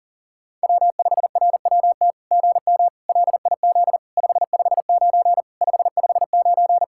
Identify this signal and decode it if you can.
Morse code, heard as a keyed tone, about 37 wpm, 700 Hz. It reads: W5RWT GM LIZ 559 559
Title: SOTA QSO Practice 40wpm